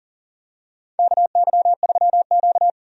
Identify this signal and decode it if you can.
Morse code, heard as a keyed tone, about 40 wpm, 700 Hz. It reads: KY3Q
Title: USA Call Signs - 4 Characters - 40wpm